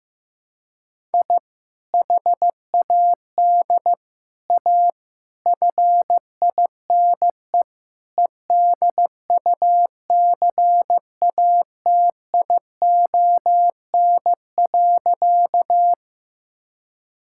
Morse code, heard as a keyed tone, about 15 wpm, 700 Hz. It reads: I HAD A FINE EDUCATION.
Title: Sentences from Top 1000 Words - Review Part 6 - 15wpm